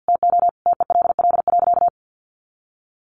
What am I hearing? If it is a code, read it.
Morse code, heard as a keyed tone, about 50 wpm, 700 Hz. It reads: TO TELL.
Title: Sentences from Top 200 Words 50wpm